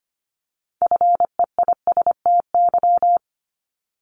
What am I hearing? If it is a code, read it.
Morse code, heard as a keyed tone, about 25 wpm, 700 Hz. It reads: FEISTY
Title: Multi-letter Phonemes 25wpm